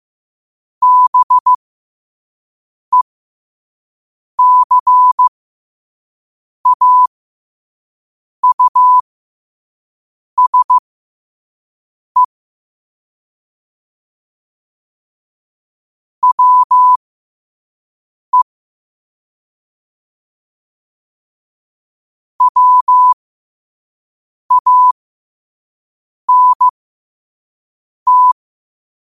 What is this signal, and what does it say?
Morse code, heard as a keyed tone, about 15 wpm, spaced as for 5 wpm, 1000 Hz. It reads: BECAUSE WE WANT
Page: 5 WPM morse code quotes